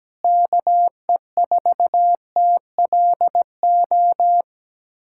Morse code, heard as a keyed tone, about 17 wpm, 700 Hz. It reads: KE4TLO
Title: USA Callsign DE Callsign 17wpm